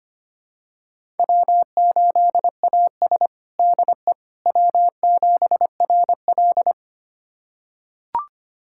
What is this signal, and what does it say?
Morse code, heard as a keyed tone, about 25 wpm, 700 Hz. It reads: W8AS DE W7RL
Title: USA Callsign DE Callsign - 4 Characters - 25wpm